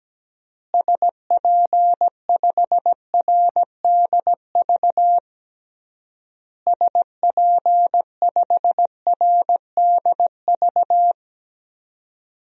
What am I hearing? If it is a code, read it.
Morse code, heard as a keyed tone, about 17 wpm, 700 Hz. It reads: SP5RDV SP5RDV
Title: DXCC-100 Call Signs - 6 Characters - Repeated 3x - 17wpm